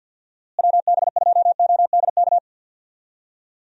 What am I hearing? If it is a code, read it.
Morse code, heard as a keyed tone, about 52 wpm, 700 Hz. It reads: WB2ODK
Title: USA Call Signs - 6 Characters - Repeated 2x - 50wpm